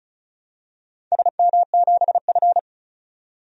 Morse code, heard as a keyed tone, about 36 wpm, 700 Hz. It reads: SM7F
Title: DXCC-100 Call Signs - 4 Characters - 35wpm